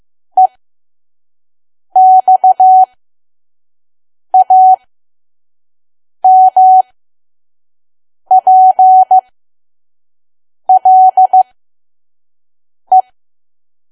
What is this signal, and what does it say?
Morse code, heard as a keyed tone, about 15 wpm, spaced as for 5 wpm, 750 Hz. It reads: EXAMPLE
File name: Archive / 5 WPM files / PM